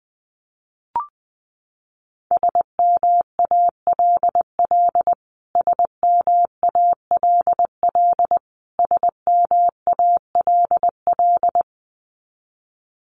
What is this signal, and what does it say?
Morse code, heard as a keyed tone, about 20 wpm, 700 Hz. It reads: SMALL SMALL SMALL
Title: Top 200 Words Repeated 20wpm